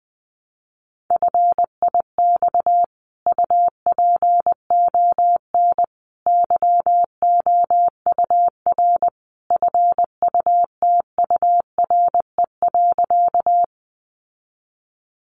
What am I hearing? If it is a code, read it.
Morse code, heard as a keyed tone, about 20 wpm, 700 Hz. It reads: FIX UPON YOUR FUTURE.
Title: Sentences from Top 1400 Words 20wpm